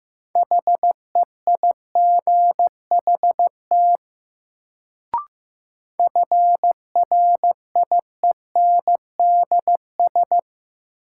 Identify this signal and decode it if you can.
Morse code, heard as a keyed tone, about 15 wpm, 700 Hz. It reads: HEIGHT FRIENDS
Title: Sentences from Top 1600 Words - New Word Drill Alt - 15wpm